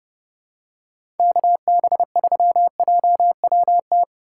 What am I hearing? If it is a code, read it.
Morse code, heard as a keyed tone, about 30 wpm, 700 Hz. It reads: KB3JWT
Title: USA Call Signs - 6 Characters - 3x Character Spacing - 30wpm